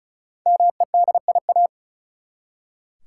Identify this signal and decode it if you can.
Morse code, heard as a keyed tone, about 34 wpm, 700 Hz. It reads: MEDIA
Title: Five Letter Words - 35wpm